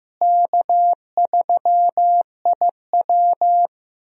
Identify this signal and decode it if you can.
Morse code, heard as a keyed tone, about 16 wpm, 700 Hz. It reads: K3IW
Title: USA Call Signs - 4 Characters - Repeated 3x - 15wpm